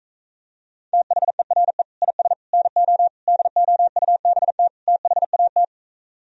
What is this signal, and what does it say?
Morse code, heard as a keyed tone, about 35 wpm, 700 Hz. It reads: THERE IS NO DOUBT THAT
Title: Speed-Racing - Sets of 5 Words 28wpm